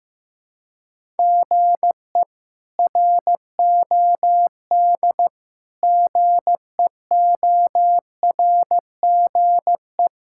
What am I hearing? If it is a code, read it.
Morse code, heard as a keyed tone, about 15 wpm, 700 Hz. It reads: GE ROD GEORGE
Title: K1USN SST QSO Practice - 15wpm